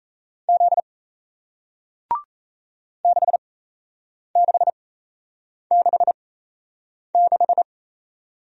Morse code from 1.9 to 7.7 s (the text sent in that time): 6666